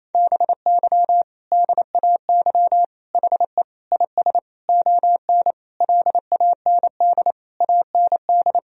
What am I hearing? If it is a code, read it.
Morse code, heard as a keyed tone, about 28 wpm, 700 Hz. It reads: BY DAY HE IS ON LAND AND
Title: Sentences from Top 1000 Words - Review Part 1 - 28wpm